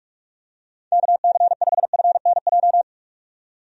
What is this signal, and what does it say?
Morse code, heard as a keyed tone, about 46 wpm, 700 Hz. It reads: KC5FNJ
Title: USA Call Signs - 6 Characters - 45wpm